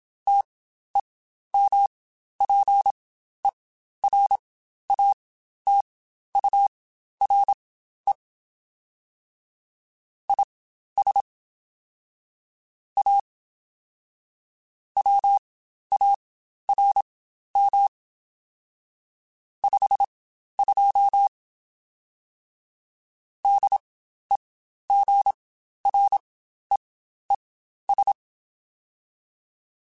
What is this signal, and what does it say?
Morse code, heard as a keyed tone, about 27 wpm, spaced as for 12 wpm, 800 Hz. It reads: TEMPERATURE IS A WARM 52 DEGREES
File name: Spaced Characters / Spaced Characters-mp3s / Weather 4 10 sp @25